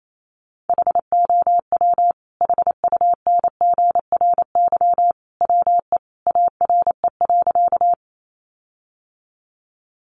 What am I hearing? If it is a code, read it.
Morse code, heard as a keyed tone, about 28 wpm, 700 Hz. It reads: HOW HUNGRY WE ARE.
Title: Sentences from Top 1400 Words 28wpm